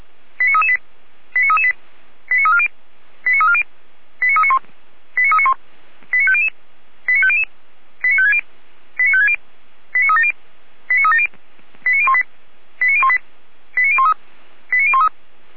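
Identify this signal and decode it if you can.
Five-tone selective call (ZVEI), heard as quick runs of five tones, tones of about 70 ms. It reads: 89338 89338 89340 89340 89391 89391 89500 89500 89560 89560 89370 89370 89018 89018 89013 89013
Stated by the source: ZVEI five-tone call received with an SDR.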